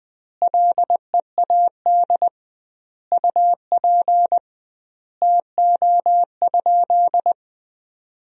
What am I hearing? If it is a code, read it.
Morse code, heard as a keyed tone, about 20 wpm, 700 Hz. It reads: LEAD UP TO?
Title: 2x Word-Spacing - Sentences from Top 500 Words - 20wpm